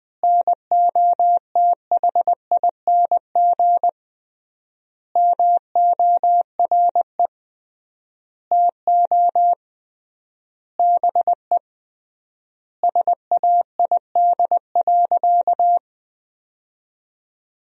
Morse code, heard as a keyed tone, about 20 wpm, 700 Hz. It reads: NOTHING MORE TO BE SAID.
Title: 3x Word-Spacing - Sentences from Top 500 Words - 20wpm